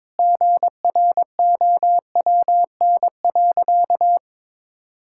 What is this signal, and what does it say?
Morse code, heard as a keyed tone, about 22 wpm, 700 Hz. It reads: GROWN.
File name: practice / 22wpm / Sentences with 4-Word Phrases - 22wpm